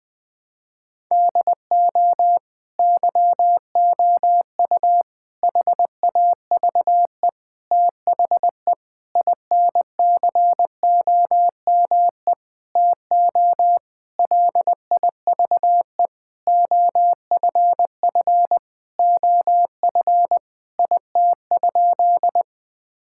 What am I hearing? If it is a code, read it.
Morse code, heard as a keyed tone, about 21 wpm, 700 Hz. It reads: DO YOU HAVE THE INCOME TO LIVE OFF OF IT?
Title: Sentences from Top 1000 Words - Review Part 5 - 20wpm